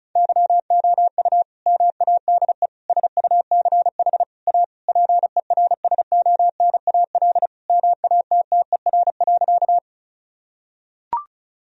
Morse code, heard as a keyed tone, about 36 wpm, 700 Hz. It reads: YOU MADE SUCH A PERSONAL MATTER.
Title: Sentences from Top 1000 Words - Review Part 4 - 35wpm